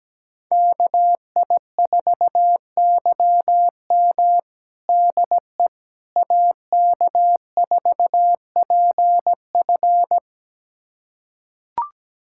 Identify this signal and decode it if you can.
Morse code, heard as a keyed tone, about 17 wpm, 700 Hz. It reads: KI4YM DE AK4PF
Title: USA Callsign DE Callsign - 5 Characters - 17wpm